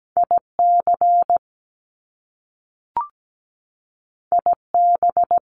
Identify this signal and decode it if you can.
Morse code, heard as a keyed tone, about 17 wpm, 700 Hz. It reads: IC IB
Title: Top Two Letter Combinations in Words 17wpm